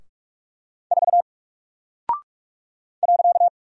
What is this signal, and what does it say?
Morse code, heard as a keyed tone, about 44 wpm, 700 Hz. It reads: 4 .
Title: Single Character (Letter:Number:Punctuation:Prosign) 45wpm